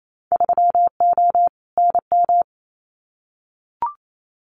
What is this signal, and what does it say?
Morse code, heard as a keyed tone, about 28 wpm, 700 Hz. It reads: 3O NM
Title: Winter Field Day Exchanges Encore 28wpm